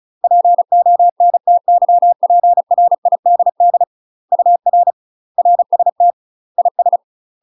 Morse code, heard as a keyed tone, about 33 wpm, 700 Hz. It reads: PONTYPRIDD UR RST IS
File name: UK MASTER QSOs TRAINING / 35WPM / UK MASTER QSOs 0036